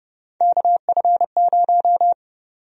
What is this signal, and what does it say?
Morse code, heard as a keyed tone, about 31 wpm, 700 Hz. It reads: KF0
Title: USA Call Sign Prefixes - 30wpm